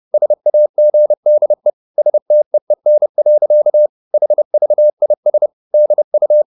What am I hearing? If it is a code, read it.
Morse code, heard as a keyed tone, about 30 wpm, 600 Hz. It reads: SAGDE STEEN. HVIS DU
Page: Ferie 30 wpm